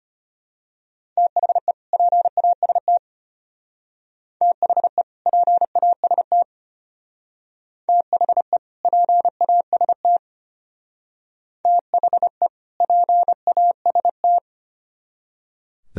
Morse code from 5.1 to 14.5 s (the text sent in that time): PAST THE PAST THE PAST